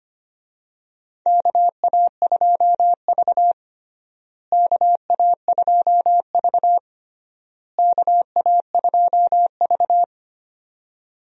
Morse code, heard as a keyed tone, about 25 wpm, 700 Hz. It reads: KA2V KA2V KA2V
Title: USA Call Signs - 4 Characters - Repeated 3x - 25wpm